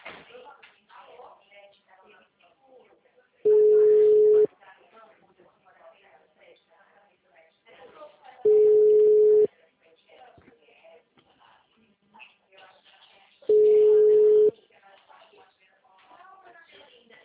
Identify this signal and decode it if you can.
Ringback tone (Europe, 425 Hz), with speech.
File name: out- -2024- - - .50039